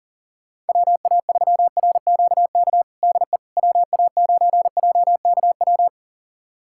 Morse code, heard as a keyed tone, about 41 wpm, 690 Hz. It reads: WA3RQK DE WA9JKW
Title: USA Callsign DE Callsign - 6 Characters - 40wpm